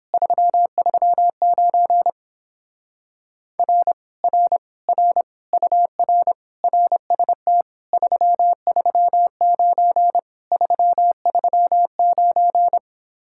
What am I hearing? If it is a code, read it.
Morse code, heard as a keyed tone, about 31 wpm, 700 Hz. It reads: 339 R R R UR RST 339 339